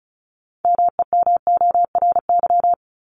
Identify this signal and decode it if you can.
Morse code, heard as a keyed tone, about 34 wpm, 700 Hz. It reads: MEMORY
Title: Top 1251-1500 Words 35wpm